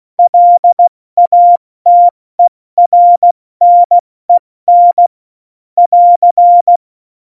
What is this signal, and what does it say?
Morse code, heard as a keyed tone, about 16 wpm, 700 Hz. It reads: LATERNEN <AR>
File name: Morsekurs / Witze / Witz 25. September 2018, 70 F80